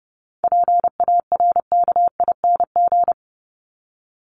Morse code, heard as a keyed tone, about 30 wpm, 700 Hz. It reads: PARKING
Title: Top 1501-1750 Words 30wpm